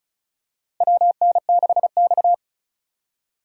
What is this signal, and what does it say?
Morse code, heard as a keyed tone, about 34 wpm, 700 Hz. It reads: WN6X
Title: USA Call Signs - 4 Characters - Repeated 3x - 35wpm